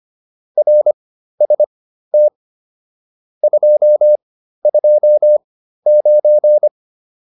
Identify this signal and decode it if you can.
Morse code, heard as a keyed tone, about 25 wpm, spaced as for 13 wpm, 600 Hz. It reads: RST 229
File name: QSO 407 13